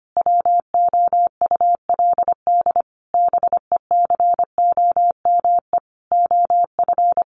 Morse code, heard as a keyed tone, about 25 wpm, 700 Hz. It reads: WOULD BECOME OF
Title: Sentences from Top 600 Words 25wpm